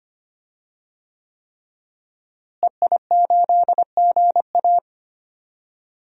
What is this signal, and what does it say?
Morse code, heard as a keyed tone, about 25 wpm, 700 Hz. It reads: EI8GA
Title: DXCC-100 Call Signs - 5 Characters - Repeated 3x - 25wpm